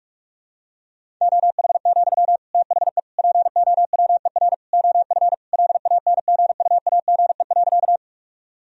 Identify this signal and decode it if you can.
Morse code, heard as a keyed tone, about 44 wpm, 700 Hz. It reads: OH, THE POWER OF LANGUAGE.
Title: Sentences from Top 1000 Words - Review Part 3 - 45wpm